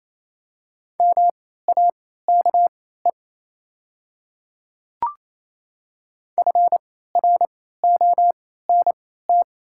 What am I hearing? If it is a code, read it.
Morse code, heard as a keyed tone, about 27 wpm, spaced as for 16 wpm, 700 Hz. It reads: MAKE FRONT
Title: 3x Character Spacing - Top 500 Words - 28wpm